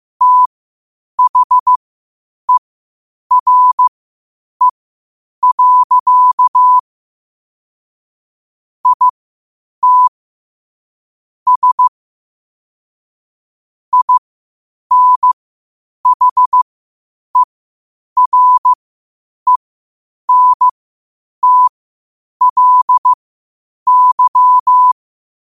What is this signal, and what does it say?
Morse code, heard as a keyed tone, about 15 wpm, spaced as for 8 wpm, 1000 Hz. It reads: THERE. IT S INHERENTLY
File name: qotdfiles / 8wpm